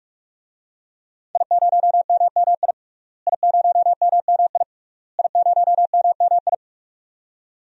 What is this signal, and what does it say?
Morse code, heard as a keyed tone, about 46 wpm, 700 Hz. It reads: I0MMI I0MMI I0MMI
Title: DXCC-100 Call Signs - 5 Characters - Repeated 3x - 45wpm